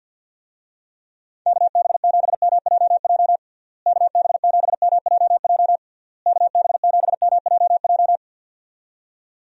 Morse code, heard as a keyed tone, about 51 wpm, 700 Hz. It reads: KB7MJJ KB7MJJ KB7MJJ
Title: USA Call Signs - 6 Characters - Repeated 3x - 50wpm